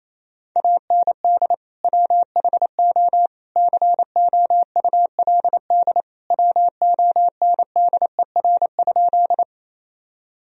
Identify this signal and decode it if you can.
Morse code, heard as a keyed tone, about 28 wpm, 700 Hz. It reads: AND WHO COULD WONDER?
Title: Sentences from Top 1000 Words - Review Part 3 - 28wpm